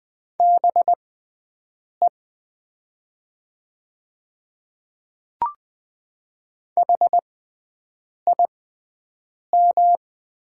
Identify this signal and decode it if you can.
Morse code, heard as a keyed tone, about 20 wpm, spaced as for 7 wpm, 700 Hz. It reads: BE HIM
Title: 6x Character Spacing - Top 100 Words - 20wpm